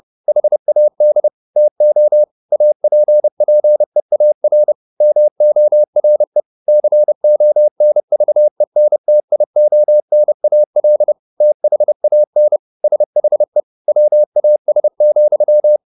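Morse code, heard as a keyed tone, about 30 wpm, 600 Hz. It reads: HAD TO APPEAR MORE CONVENTIONAL THAN SHE WAS,
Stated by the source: Morse code at 30 wpm, the novel Never by Ken Follett in English, from CW med Gnister.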